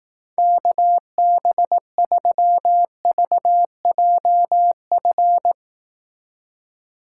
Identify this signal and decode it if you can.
Morse code, heard as a keyed tone, about 18 wpm, 700 Hz. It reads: KB3VJF
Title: Speed Racing - USA 2x3 Call Signs 15wpm